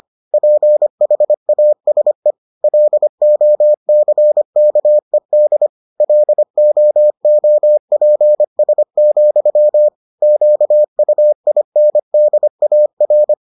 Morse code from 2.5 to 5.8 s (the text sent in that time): LOCKED